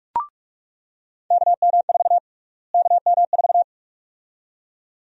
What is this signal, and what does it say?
Morse code, heard as a keyed tone, about 42 wpm, 700 Hz. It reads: KM4 KM4
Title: USA Call Sign Prefixes - Repeated 2x - 45wpm